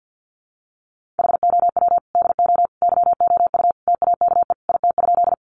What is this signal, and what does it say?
Morse code, heard as a keyed tone, about 48 wpm, 700 Hz. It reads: HOW DO YOU TAKE IT?
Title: Sentences from Top 200 Words 50wpm